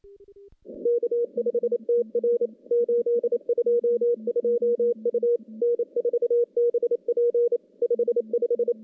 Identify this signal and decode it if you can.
Morse code, heard as a keyed tone, about 28 wpm, 480 Hz. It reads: K5TR 822 U N4BP 55
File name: contests / CW-SS-k5tr- -1200